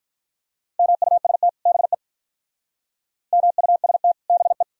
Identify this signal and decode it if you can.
Morse code, heard as a keyed tone, about 43 wpm, 700 Hz. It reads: MUST BE MUST BE
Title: Speed-Racing - Sets of 2 Words 35wpm